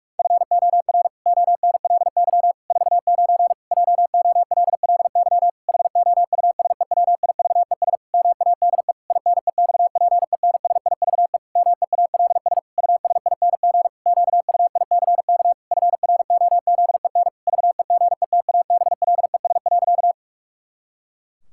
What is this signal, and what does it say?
Morse code, heard as a keyed tone, about 41 wpm, 700 Hz. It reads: FOR ONLY 49 JOLLY HOUSEWIVES MADE INEXPENSIVE MEALS USING QUICK FROZEN VEGETABLES.
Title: Pangram Challenge v2 - 45wpm